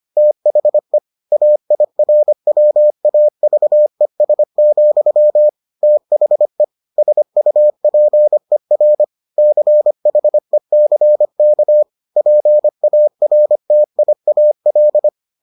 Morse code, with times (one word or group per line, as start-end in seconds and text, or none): 0.2-1.0 THE
1.3-5.5 AIRWAVES,
5.8-6.6 THE
7.0-9.0 SUPER
9.4-11.8 CHECK
12.2-15.1 PARTIAL